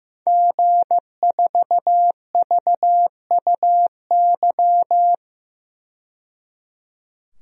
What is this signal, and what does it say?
Morse code, heard as a keyed tone, about 15 wpm, 700 Hz. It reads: G4VUY
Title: DXCC-100 Call Signs - 5 Characters - 15wpm